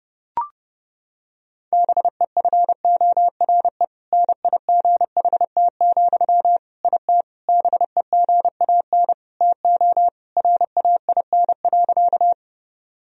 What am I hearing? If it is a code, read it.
Morse code, heard as a keyed tone, about 30 wpm, 700 Hz. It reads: BEFORE NIGHT, IT BEGAN TO RAIN.
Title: Sentences from Top 1000 Words - Review Part 1 - 30wpm